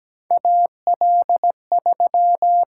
Morse code, heard as a keyed tone, about 17 wpm, 700 Hz. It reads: AL3
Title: USA Call Signs - 6x Character Spacing - 17wpm